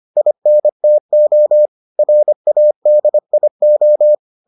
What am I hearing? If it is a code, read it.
Morse code, heard as a keyed tone, about 25 wpm, 600 Hz. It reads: INTO RADIO